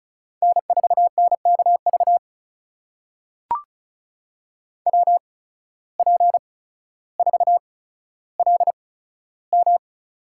Morse code, heard as a keyed tone, about 36 wpm, spaced as for 15 wpm, 700 Hz. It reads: N4NKV WP4LM
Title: USA Call Signs - 5 Characters - 8x Character Spacing - 35wpm